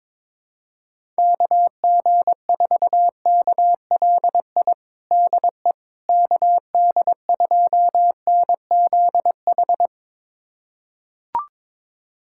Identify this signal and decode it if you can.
Morse code, heard as a keyed tone, about 21 wpm, 700 Hz. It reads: KG4KLI DE KD2NZH
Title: USA Callsign DE Callsign 22wpm